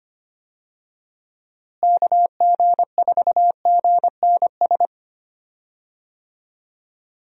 Morse code, heard as a keyed tone, about 25 wpm, 700 Hz. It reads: KG4GNS
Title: USA Call Signs - 6 Characters - Repeated 2x - 25wpm